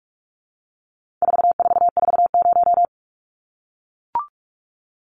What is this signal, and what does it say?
Morse code, heard as a keyed tone, about 44 wpm, 700 Hz. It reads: 4440
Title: Numbers - 4-Digit 45wpm